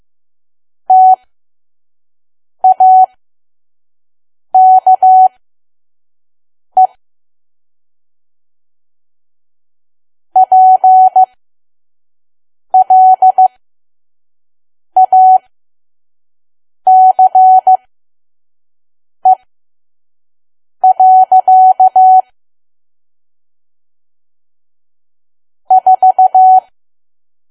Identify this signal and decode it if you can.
Morse code, heard as a keyed tone, about 15 wpm, spaced as for 5 wpm, 750 Hz. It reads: TAKE PLACE. 4
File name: Archive / 5 WPM files / PM